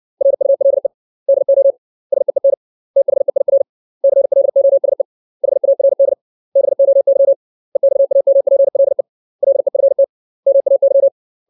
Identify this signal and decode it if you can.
Morse code, heard as a keyed tone, about 64 wpm, 550 Hz. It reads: RULE DO SEEN THEIR CLOSE HARD BOY EXAMPLE LEFT MAY
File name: 632 words / 632 Words 60WPM